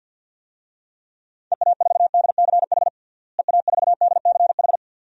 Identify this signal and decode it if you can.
Morse code, heard as a keyed tone, about 50 wpm, 700 Hz. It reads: EA4DCH EA4DCH
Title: DXCC-100 Call Signs - 6 Characters - Repeated 2x - 50wpm